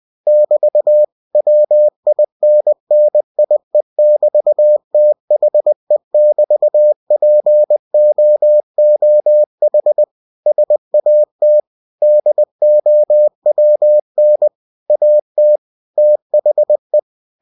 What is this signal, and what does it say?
Morse code, heard as a keyed tone, about 20 wpm, 600 Hz. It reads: <BT> WINNIE<BT>THE<BT>POOH SAT DOWN AT THE
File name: cw-books / winnie-the-pooh / winnie-20wpm-page-0002